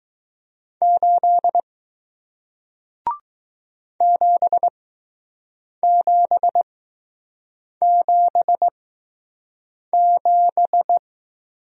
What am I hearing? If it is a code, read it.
Morse code, heard as a keyed tone, about 20 wpm, spaced as for 6 wpm, 700 Hz. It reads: 8 7777